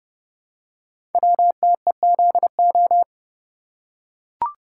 Morse code, heard as a keyed tone, about 30 wpm, 690 Hz. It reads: WTEZO
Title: Code Groups - 5 Letters - 30wpm